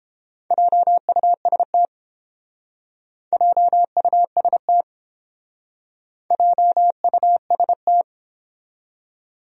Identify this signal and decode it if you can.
Morse code, heard as a keyed tone, about 29 wpm, 700 Hz. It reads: JUST JUST JUST